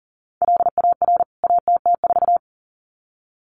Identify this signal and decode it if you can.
Morse code, heard as a keyed tone, about 38 wpm, 700 Hz. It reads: LAR ATT4
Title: CWT Exchange 40wpm